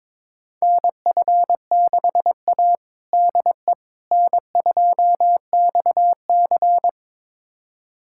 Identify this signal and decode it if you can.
Morse code, heard as a keyed tone, about 22 wpm, 700 Hz. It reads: NF6A DE N2XC
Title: USA Callsign DE Callsign - 4 Characters - 22wpm